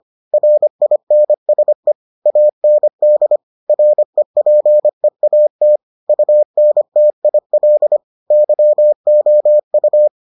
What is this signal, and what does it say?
Morse code, heard as a keyed tone, about 25 wpm, 600 Hz. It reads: RINSE AND REPEAT UNTIL YOU